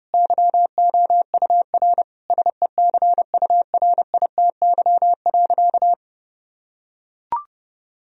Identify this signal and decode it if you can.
Morse code, heard as a keyed tone, about 32 wpm, 700 Hz. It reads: YOUR SECURITY.
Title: Sentences from Top 1000 Words - Review Part 4 - 30wpm